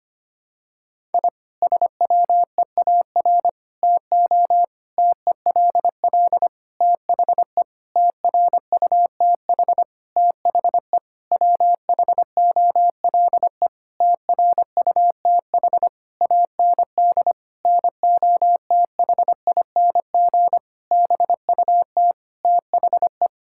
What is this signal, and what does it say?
Morse code, heard as a keyed tone, about 25 wpm, 700 Hz. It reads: I SWEAR TO TELL THE TRUTH THE WHOLE TRUTH AND NOTHING BUT THE
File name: practice / 25wpm / Sentences with 9-Word Phrases - 25wpm